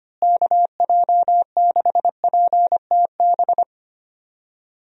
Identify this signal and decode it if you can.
Morse code, heard as a keyed tone, about 25 wpm, 700 Hz. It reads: KJ6PTB
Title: USA Call Signs - 6 Characters - Repeated 3x - 25wpm